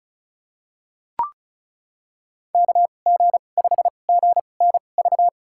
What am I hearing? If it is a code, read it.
Morse code, heard as a keyed tone, about 34 wpm, spaced as for 25 wpm, 700 Hz. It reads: KG5GNV
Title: USA Call Signs - 2x Character Spacing - 35wpm